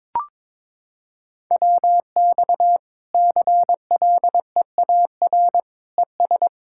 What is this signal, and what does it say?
Morse code, heard as a keyed tone, about 22 wpm, 700 Hz. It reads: WX CLEAR ES
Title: K1USN SST QSO Practice 22wpm